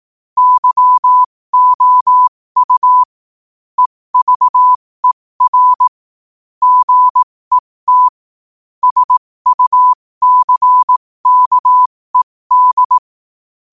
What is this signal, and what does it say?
Morse code, heard as a keyed tone, about 18 wpm, spaced as for 15 wpm, 1000 Hz. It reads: YOU EVER GET SUCKED
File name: 15wpm